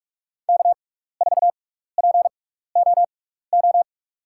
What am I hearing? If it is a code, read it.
Morse code, heard as a keyed tone, about 46 wpm, spaced as for 16 wpm, 700 Hz. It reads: K4POO
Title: USA Call Signs - 5 Characters - 6x Character Spacing - 45wpm